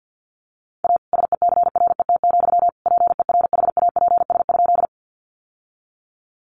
Morse code, heard as a keyed tone, about 50 wpm, 700 Hz. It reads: A SECRET, PERHAPS?
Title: Sentences from Top 1200 Words 50wpm